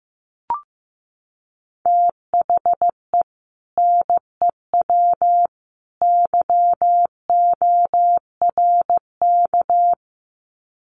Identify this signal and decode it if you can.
Morse code, heard as a keyed tone, about 15 wpm, 700 Hz. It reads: THE NEW YORK
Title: Sets of 3 Words 15wpm